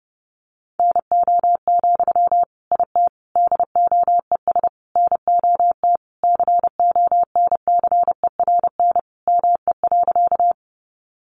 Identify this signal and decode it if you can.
Morse code, heard as a keyed tone, about 30 wpm, 700 Hz. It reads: NO, IT DOES NOT CONCERN ME.
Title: Sentences from Top 1000 Words 30wpm